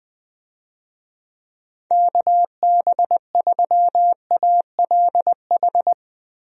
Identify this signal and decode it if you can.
Morse code, heard as a keyed tone, about 20 wpm, 700 Hz. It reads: KB3ALH
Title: USA Call Signs - 6 Characters - Repeated 3x - 20wpm